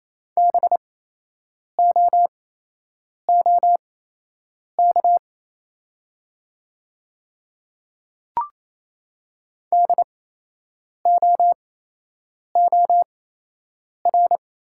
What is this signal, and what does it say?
Morse code, heard as a keyed tone, about 27 wpm, spaced as for 8 wpm, 690 Hz. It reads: BOOK DOOR
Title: 8x Character Spacing - Top 300 Words - 28wpm